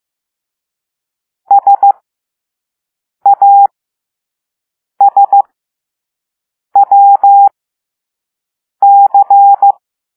Morse code, heard as a keyed tone, about 15 wpm, spaced as for 5 wpm, 800 Hz. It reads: SASWC